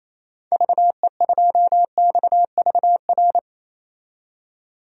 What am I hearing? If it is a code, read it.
Morse code, heard as a keyed tone, about 27 wpm, 700 Hz. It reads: VE2XVR
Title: DXCC-100 Call Signs - 6 Characters - 28wpm